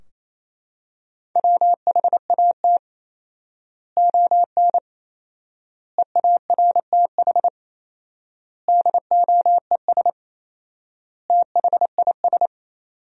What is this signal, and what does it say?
Morse code, heard as a keyed tone, about 28 wpm, 700 Hz. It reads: WHAT ON EARTH DOES THIS
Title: 4x Word-Spacing - Sentences from Top 200 Words - 28wpm